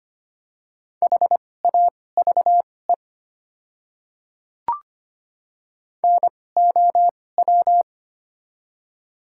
Morse code, heard as a keyed tone, about 25 wpm, spaced as for 18 wpm, 690 Hz. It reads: HAVE NOW
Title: 2x Character Spacing - Top 100 Words - 25wpm